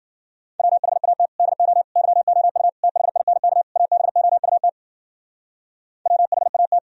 Morse code, heard as a keyed tone, about 54 wpm, 700 Hz. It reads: WHAT DO YOU THINK ABOUT WHAT